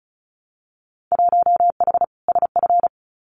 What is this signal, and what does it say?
Morse code, heard as a keyed tone, about 36 wpm, 700 Hz. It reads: 1H SF
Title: Winter Field Day Exchanges 35wpm